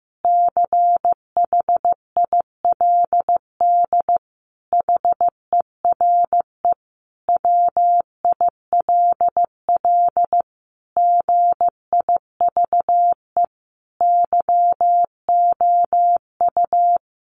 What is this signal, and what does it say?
Morse code, heard as a keyed tone, about 15 wpm, 700 Hz. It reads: CHILD HERE WILL GIVE YOU
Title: Sentences from Top 1100 Words 15wpm